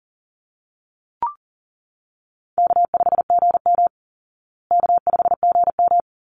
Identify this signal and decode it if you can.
Morse code, heard as a keyed tone, about 40 wpm, 700 Hz. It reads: K5GM K5GM
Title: Easy Call Signs Repeated 2x - 40wpm